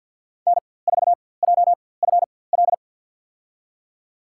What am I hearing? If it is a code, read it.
Morse code, heard as a keyed tone, about 49 wpm, spaced as for 23 wpm, 690 Hz. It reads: N4JFL
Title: USA Call Signs - 5 Characters - 4x Character Spacing - 50wpm